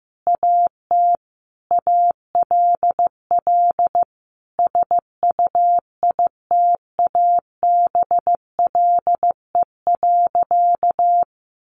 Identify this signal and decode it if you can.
Morse code, heard as a keyed tone, about 14 wpm, 700 Hz. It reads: AT ALL SUITABLE.
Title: Sentences from Top 1400 Words 15wpm